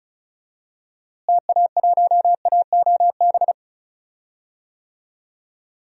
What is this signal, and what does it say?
Morse code, heard as a keyed tone, about 36 wpm, 700 Hz. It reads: TA1AOB
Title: DXCC-100 Call Signs - 6 Characters - Repeated 2x - 35wpm